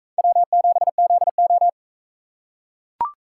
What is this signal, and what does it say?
Morse code, heard as a keyed tone, about 42 wpm, 700 Hz. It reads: W7ZO